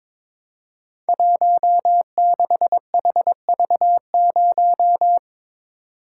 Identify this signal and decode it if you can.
Morse code, heard as a keyed tone, about 23 wpm, 700 Hz. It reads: 16HV0
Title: Code Groups - 5 Letter:Numbers - 22wpm